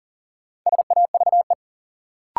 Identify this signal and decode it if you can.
Morse code, heard as a keyed tone, about 40 wpm, 700 Hz. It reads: SAVE
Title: Sentences from Top 1100 Words - New Word Drill - 40wpm